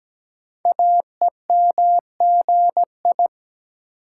Morse code, heard as a keyed tone, about 17 wpm, 700 Hz. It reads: AEMGI
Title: Code Groups - 5 Letters - 17wpm